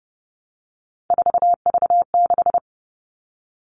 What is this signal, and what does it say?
Morse code, heard as a keyed tone, about 30 wpm, 700 Hz. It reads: 4V6
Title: All Cut Numbers - 3-Digit 30wpm